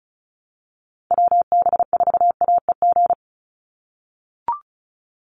Morse code, heard as a keyed tone, about 36 wpm, 700 Hz. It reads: WB4AEG
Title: Call Signs - 35wpm